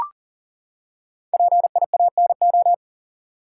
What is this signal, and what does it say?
Morse code, heard as a keyed tone, about 40 wpm, 700 Hz. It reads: PIANO
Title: Five Letter Words - 40wpm